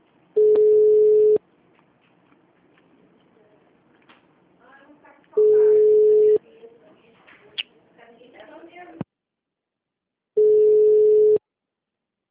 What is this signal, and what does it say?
Ringback tone (Europe, 425 Hz), with speech.